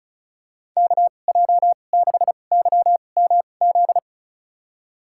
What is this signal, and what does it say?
Morse code, heard as a keyed tone, about 34 wpm, spaced as for 25 wpm, 700 Hz. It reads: KJ6YMZ
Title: USA Call Signs - 6 Characters - 2x Character Spacing - 35wpm